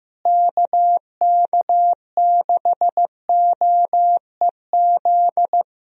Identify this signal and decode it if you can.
Morse code, heard as a keyed tone, about 15 wpm, 700 Hz. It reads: KK6OEZ
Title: USA Call Signs - 6 Characters - Repeated 3x - 15wpm